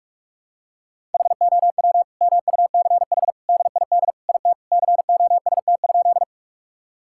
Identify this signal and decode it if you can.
Morse code, heard as a keyed tone, about 42 wpm, 700 Hz. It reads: HOW MUCH DID IT COST?
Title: Sentences from Top 1000 Words - Review Part 5 - 45wpm